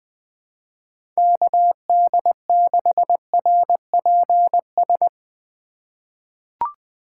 Morse code, heard as a keyed tone, about 21 wpm, 700 Hz. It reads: KD6RPS
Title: USA Call Signs - 4x Character Spacing - 20wpm